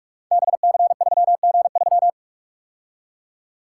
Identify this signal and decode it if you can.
Morse code, heard as a keyed tone, about 46 wpm, 700 Hz. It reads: BC3G3
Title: Code Groups - 5 Letter:Numbers - 45wpm